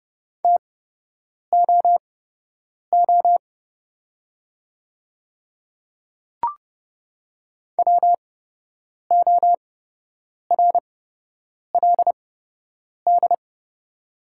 Morse code, heard as a keyed tone, about 30 wpm, spaced as for 8 wpm, 700 Hz. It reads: TOO WORLD
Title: 8x Character Spacing - Top 200 Words - 30wpm